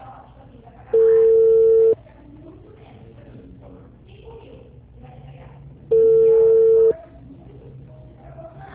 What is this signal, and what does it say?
Ringback tone (China, 450 Hz).
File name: out- -2018- - - .4